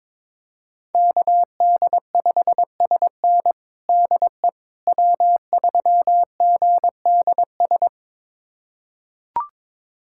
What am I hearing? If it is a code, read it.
Morse code, heard as a keyed tone, about 22 wpm, 700 Hz. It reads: KD5SN DE W3GDS
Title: USA Callsign DE Callsign 22wpm